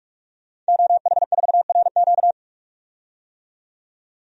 Morse code, heard as a keyed tone, about 46 wpm, 700 Hz. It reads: OH4RQ
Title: DXCC-100 Call Signs - 5 Characters - 45wpm